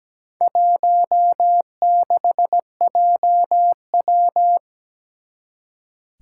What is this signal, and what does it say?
Morse code, heard as a keyed tone, about 17 wpm, 700 Hz. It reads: 16JW
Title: Four Challenging Letter-Number Combinations 17wpm